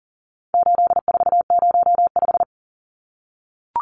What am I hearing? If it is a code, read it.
Morse code, heard as a keyed tone, about 40 wpm, 700 Hz. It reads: 8405
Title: Numbers - 4-Digit 40wpm